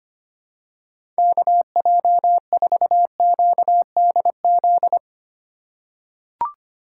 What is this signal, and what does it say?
Morse code, heard as a keyed tone, about 25 wpm, 700 Hz. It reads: KJ4QDZ
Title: USA Call Signs - 6 Characters - 4x Character Spacing - 25wpm